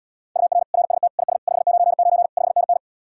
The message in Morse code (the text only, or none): UR RST IS 599 5NN